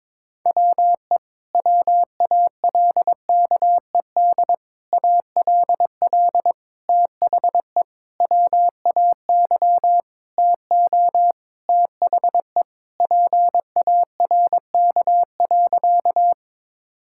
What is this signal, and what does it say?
Morse code, heard as a keyed tone, about 22 wpm, 700 Hz. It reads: WE WALKED ALL THE WAY TO THE PARK.
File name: practice / 22wpm / Sentences with 4-Word Phrases - 22wpm